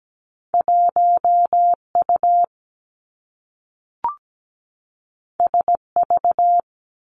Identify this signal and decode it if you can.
Morse code, heard as a keyed tone, about 17 wpm, 700 Hz. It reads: 1 U S V
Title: Two Challenging Letter-Number Combinations 17wpm